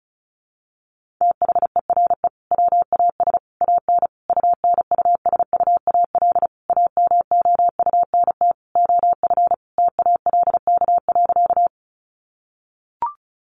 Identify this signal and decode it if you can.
Morse code, heard as a keyed tone, about 34 wpm, 700 Hz. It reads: THERE WAS AN UNUSUAL AMOUNT OF TALK.
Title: Sentences from Top 1400 Words 35wpm